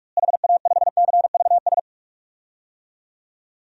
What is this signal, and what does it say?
Morse code, heard as a keyed tone, about 43 wpm, 700 Hz. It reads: HA5CVS
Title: DXCC-100 Call Signs - 6 Characters - Repeated 2x - 45wpm